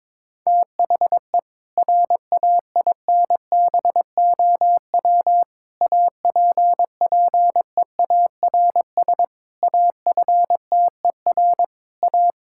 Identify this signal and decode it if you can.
Morse code, heard as a keyed tone, about 22 wpm, 700 Hz. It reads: THE RAINBOW APPEARS AFTER A
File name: Difficult Sentences with Hints - 22wpm